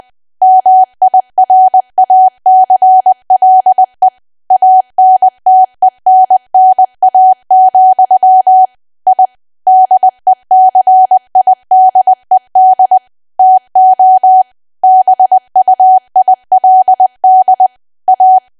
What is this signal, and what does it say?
Morse code, heard as a keyed tone, about 20 wpm, 750 Hz. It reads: MIRACLE ANTENNA, I DECIDED TO BUILD A
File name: cwa / W1AW-Level3 / 20WPM